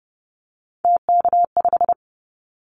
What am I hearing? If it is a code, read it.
Morse code, heard as a keyed tone, about 29 wpm, 700 Hz. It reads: TK5
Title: Callsign Prefixes for Top 100 DXCC - 30wpm